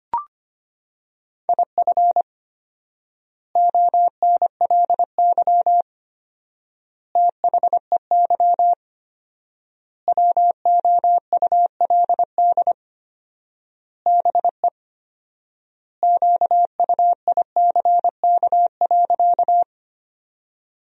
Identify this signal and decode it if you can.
Morse code, heard as a keyed tone, about 25 wpm, 700 Hz. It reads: IF ONLY THEY WOULD BE QUICK.
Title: 4x Word-Spacing - Sentences from Top 500 Words - 25wpm